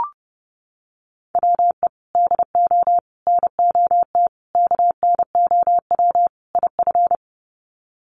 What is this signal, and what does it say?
Morse code, heard as a keyed tone, about 30 wpm, 700 Hz. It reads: WE DO NOT KNOW IF
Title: Sets of 5 Words Encore2 30wpm